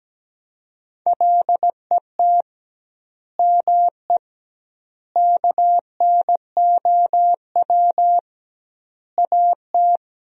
Morse code, heard as a keyed tone, about 17 wpm, 700 Hz. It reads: LET ME KNOW AT
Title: 2x Word-Spacing - Sentences from Top 300 Words - 17wpm